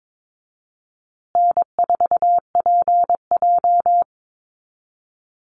Morse code, heard as a keyed tone, about 22 wpm, 700 Hz. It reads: N4PJ
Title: Easy Call Signs Repeated 3x - 22wpm